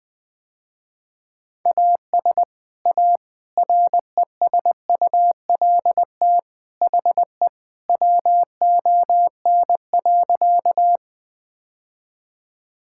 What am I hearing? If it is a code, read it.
Morse code, heard as a keyed tone, about 21 wpm, 700 Hz. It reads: AS A RESULT HE WON.
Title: Sentences with 3-Word Phrases - Spoken Beginning - 20wpm